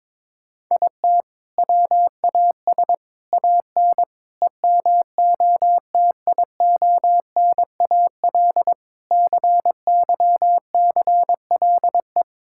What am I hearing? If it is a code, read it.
Morse code, heard as a keyed tone, about 22 wpm, 700 Hz. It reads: IT WAS AN EMOTIONAL CYCLE
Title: Sentences from Top 1000 Words - Review Part 6 - 22wpm